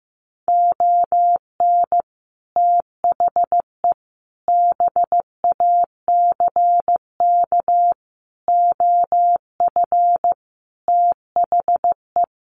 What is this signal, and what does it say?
Morse code, heard as a keyed tone, about 15 wpm, 700 Hz. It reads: ON THE BACK OF THE
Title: Sets of 5 Words 15wpm